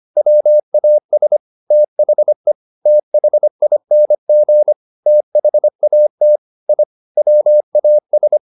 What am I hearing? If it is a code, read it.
Morse code, heard as a keyed tone, about 25 wpm, 600 Hz. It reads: WAS THE THING THAT I WAS
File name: the old man and the sea part 1 25wpm